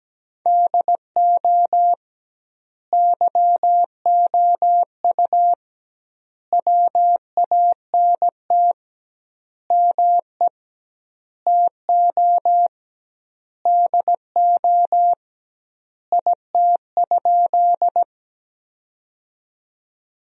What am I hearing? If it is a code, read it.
Morse code, heard as a keyed tone, about 17 wpm, 700 Hz. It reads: DO YOU WANT ME TO DO IT?
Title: 2x Word-Spacing - Sentences from Top 200 Words - 17wpm